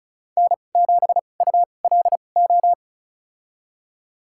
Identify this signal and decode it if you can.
Morse code, heard as a keyed tone, about 36 wpm, spaced as for 25 wpm, 700 Hz. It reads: N7ULO
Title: USA Call Signs - 5 Characters - 2x Character Spacing - 35wpm